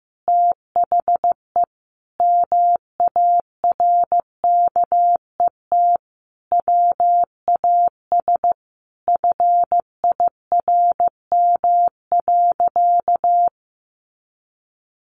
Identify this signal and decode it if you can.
Morse code, heard as a keyed tone, about 15 wpm, 700 Hz. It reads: THE MARKET WAS FIRM.
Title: Sentences from Top 1400 Words 15wpm